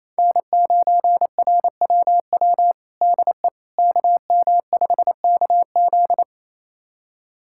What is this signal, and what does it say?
Morse code, heard as a keyed tone, about 28 wpm, 700 Hz. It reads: N9RWW DE KM5KZ
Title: USA Callsign DE Callsign - 5 Characters - 28wpm